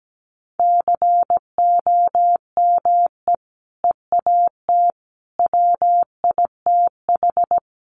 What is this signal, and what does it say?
Morse code, heard as a keyed tone, about 17 wpm, 700 Hz. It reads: COME EAT WITH
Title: Sentences from Top 300 Words 17wpm